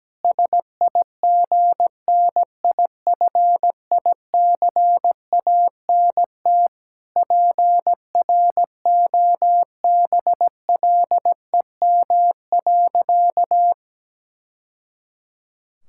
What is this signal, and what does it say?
Morse code, heard as a keyed tone, about 17 wpm, 700 Hz. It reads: SIGNIFICANT PROBLEM.
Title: Sentences from Top 1000 Words - Review Part 6 - 17wpm